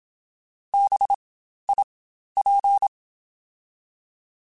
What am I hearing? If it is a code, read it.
Morse code, heard as a keyed tone, about 27 wpm, spaced as for 12 wpm, 800 Hz. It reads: BIP
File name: Spaced Alphas / Spaced Alphas mp3s / Bs sp @25